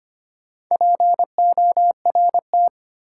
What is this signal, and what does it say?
Morse code, heard as a keyed tone, about 25 wpm, 700 Hz. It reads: PORT
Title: Sentences from Top 1000 Words - Review Part 1 - New Word Drill - 25wpm